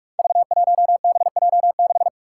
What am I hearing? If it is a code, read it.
Morse code, heard as a keyed tone, about 46 wpm, 700 Hz. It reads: V1BJ6
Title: Five Challenging Letter-Number Combinations 45wpm